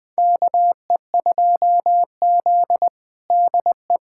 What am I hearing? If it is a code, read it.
Morse code, heard as a keyed tone, about 20 wpm, 700 Hz. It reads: KE2Z DE
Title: USA Callsign DE Callsign - 4 Characters - 20wpm